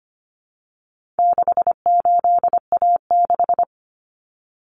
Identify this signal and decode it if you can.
Morse code, heard as a keyed tone, about 25 wpm, 700 Hz. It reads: 68A6
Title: Common Cut Numbers - 4-Digit 25wpm